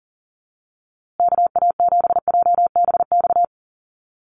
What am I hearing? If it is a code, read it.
Morse code, heard as a keyed tone, about 40 wpm, 700 Hz. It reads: KA7JBX
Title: USA Call Signs Repeated 3x 40wpm